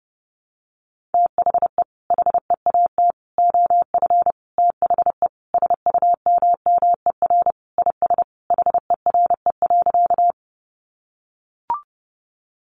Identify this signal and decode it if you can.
Morse code, heard as a keyed tone, about 28 wpm, 700 Hz. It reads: THE HEAT OF THE SUMMER IS HERE.
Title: Sentences from Top 1100 Words 30wpm